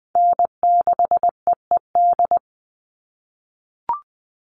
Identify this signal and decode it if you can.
Morse code, heard as a keyed tone, about 20 wpm, 700 Hz. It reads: N6EED
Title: Call Signs - 20wpm